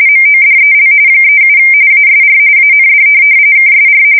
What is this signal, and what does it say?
RTTY, about 45.45 baud. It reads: MODEL 37 ASR AND KSR
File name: fsk170 / teletype-corp